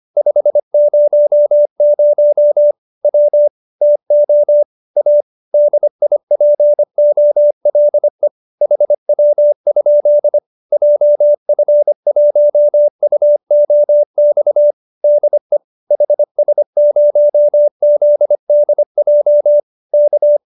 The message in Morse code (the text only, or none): 500 W TO A DIPOLE HW? JF1UOX DE HS0ZDJ K